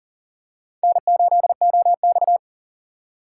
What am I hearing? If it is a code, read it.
Morse code, heard as a keyed tone, about 39 wpm, 700 Hz. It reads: N8OX
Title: USA Call Signs - 4 Characters - 2x Character Spacing - 40wpm